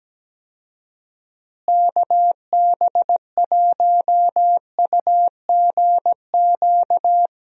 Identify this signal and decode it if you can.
Morse code, heard as a keyed tone, about 17 wpm, 700 Hz. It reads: KB1UGQ
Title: USA Call Signs - 6 Characters - Repeated 3x - 17wpm